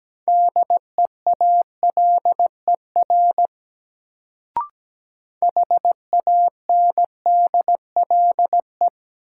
Morse code, heard as a keyed tone, about 17 wpm, 700 Hz. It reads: DEALER HANDLE
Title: Sentences from Top 1100 Words - New Word Drill - 17wpm